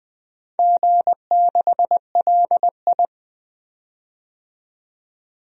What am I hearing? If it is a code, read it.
Morse code, heard as a keyed tone, about 20 wpm, 700 Hz. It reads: G6LI
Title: DXCC-100 Call Signs - 4 Characters - 20wpm